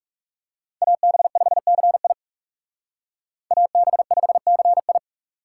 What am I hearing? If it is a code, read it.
Morse code, heard as a keyed tone, about 37 wpm, 700 Hz. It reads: AB5CI AB5CI
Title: Speed Racing - USA 2x2 Call Signs 30wpm